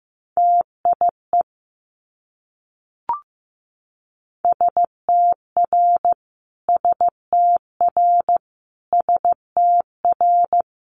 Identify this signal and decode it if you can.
Morse code, heard as a keyed tone, about 15 wpm, 700 Hz. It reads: TIE STR STR STR
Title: Multi-letter Phonemes Repeated 3x 15wpm